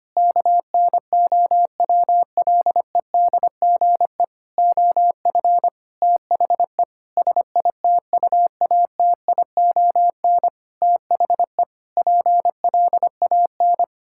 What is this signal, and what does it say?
Morse code, heard as a keyed tone, about 25 wpm, 700 Hz. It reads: KNOWLEDGE OF THE SITUATION THE PLAN
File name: practice / 25wpm / Sentences with 9-Word Phrases - 25wpm